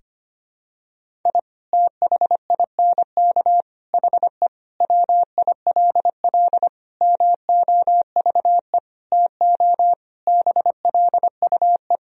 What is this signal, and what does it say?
Morse code, heard as a keyed tone, about 25 wpm, 700 Hz. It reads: I THINK HE WILL MOVE TO BLUE
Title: Sentences from Top 1000 Words - Review Part 3 - 25wpm